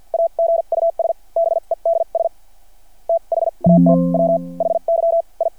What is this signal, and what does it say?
Morse code, heard as a keyed tone, about 52 wpm, spaced as for 39 wpm, 600 Hz. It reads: AGUS BEDS THAT WHY I
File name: archived / CRSnet